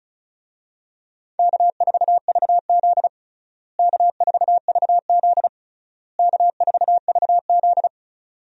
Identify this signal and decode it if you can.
Morse code, heard as a keyed tone, about 36 wpm, 700 Hz. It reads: K4VZ K4VZ K4VZ
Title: USA Call Signs - 4 Characters - Repeated 3x - 35wpm